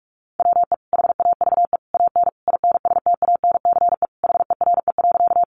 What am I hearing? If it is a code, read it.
Morse code, heard as a keyed tone, about 44 wpm, 700 Hz. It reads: WE HAVE AN INSTANCE HERE.
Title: Sentences from Top 800 Words 45wpm